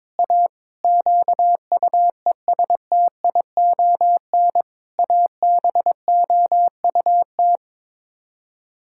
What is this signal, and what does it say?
Morse code, heard as a keyed tone, about 22 wpm, 700 Hz. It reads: A QUESTION ABOUT
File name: practice / 22wpm / Sets of 6 Words - 22wpm